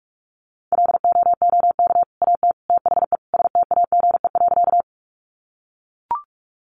Morse code, heard as a keyed tone, about 44 wpm, 700 Hz. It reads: LOOK AT THE STAGE.
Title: Sentences from Top 1200 Words 45wpm